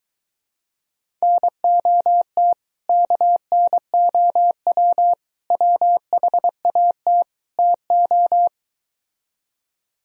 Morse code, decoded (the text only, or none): NOT KNOW WHAT TO